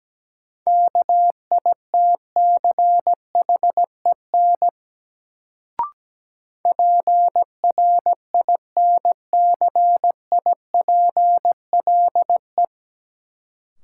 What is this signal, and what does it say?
Morse code, heard as a keyed tone, about 17 wpm, 700 Hz. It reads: KITCHEN PRINCIPLE
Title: Sentences from Top 1400 Words - New Word Drill - 17wpm